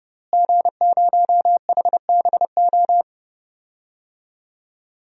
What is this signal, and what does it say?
Morse code, heard as a keyed tone, about 30 wpm, 700 Hz. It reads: G0HBO
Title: DXCC-100 Call Signs - 5 Characters - Repeated 3x - 30wpm